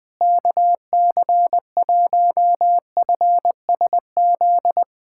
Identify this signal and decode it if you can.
Morse code, heard as a keyed tone, about 21 wpm, 700 Hz. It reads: KC1FSZ
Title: K1USN SST QSO Practice - 20wpm